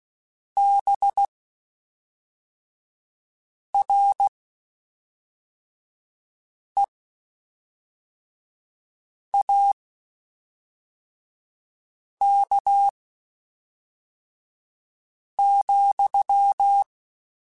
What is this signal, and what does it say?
Morse code, heard as a keyed tone, about 16 wpm, spaced as for 3 wpm, 800 Hz. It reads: BREAK,
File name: (spaced) / QSOs Spaced-mp3s / WA7QOX 1 @5 sp